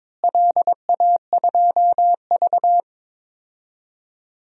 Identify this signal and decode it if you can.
Morse code, heard as a keyed tone, about 22 wpm, 700 Hz. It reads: LA2V
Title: DXCC-100 Call Signs - 4 Characters - Repeated 3x - 22wpm